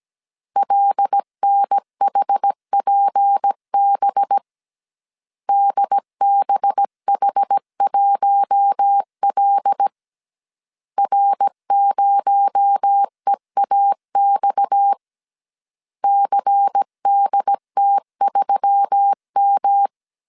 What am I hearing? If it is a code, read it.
Morse code, heard as a keyed tone, about 17 wpm, 800 Hz. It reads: LNHPB DBH1L R0EAX CDT3M
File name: CW / DET / 1 6 1500C S A 16wpm 2635